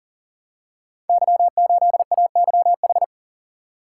Morse code, heard as a keyed tone, about 39 wpm, 700 Hz. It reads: Y8AYH
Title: Code Groups - 5 Letter:Numbers - 40wpm